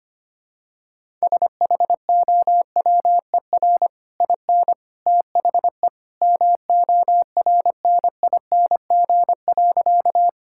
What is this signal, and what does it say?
Morse code, heard as a keyed tone, about 25 wpm, 700 Hz. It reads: SHOWER IN THE MORNING.
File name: Easy Sentences with Hints - 25wpm